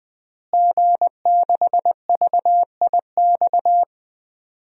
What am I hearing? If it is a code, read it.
Morse code, heard as a keyed tone, about 20 wpm, 700 Hz. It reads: G6VIX
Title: DXCC-100 Call Signs - 5 Characters - Repeated 2x - 20wpm